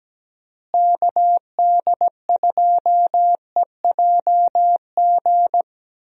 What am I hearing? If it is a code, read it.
Morse code, heard as a keyed tone, about 17 wpm, 700 Hz. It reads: KD2EJG
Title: USA Call Signs - 6 Characters - Repeated 3x - 17wpm